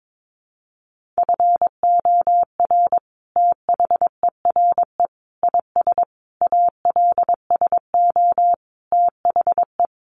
Morse code, heard as a keyed tone, about 22 wpm, 700 Hz. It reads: FOR THERE IS ALSO THE
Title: Sentences from Top 1400 Words 22wpm